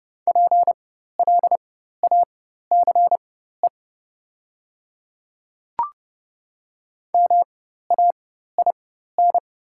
Morse code, heard as a keyed tone, about 30 wpm, spaced as for 14 wpm, 700 Hz. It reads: PLACE MAIN
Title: 4x Character Spacing - Top 300 Words - 30wpm